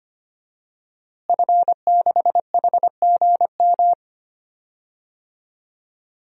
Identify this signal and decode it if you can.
Morse code, heard as a keyed tone, about 25 wpm, 700 Hz. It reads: F6HGM
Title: DXCC-100 Call Signs - 5 Characters - Repeated 2x - 25wpm